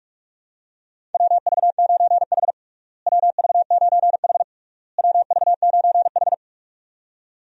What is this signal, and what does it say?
Morse code, heard as a keyed tone, about 42 wpm, 700 Hz. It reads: WV9H WV9H WV9H
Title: USA Call Signs - 4 Characters - Repeated 3x - 45wpm